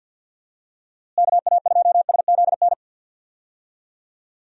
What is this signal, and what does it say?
Morse code, heard as a keyed tone, about 52 wpm, 700 Hz. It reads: KA2SZN
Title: USA Call Signs - 6 Characters - Repeated 3x - 50wpm